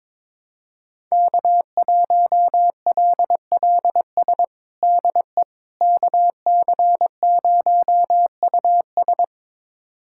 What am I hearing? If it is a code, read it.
Morse code, heard as a keyed tone, about 22 wpm, 700 Hz. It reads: K1LLS DE KC0US
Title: USA Callsign DE Callsign - 5 Characters - 22wpm